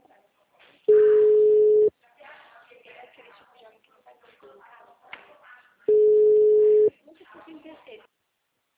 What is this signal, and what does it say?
Ringback tone (Europe, 425 Hz), with speech.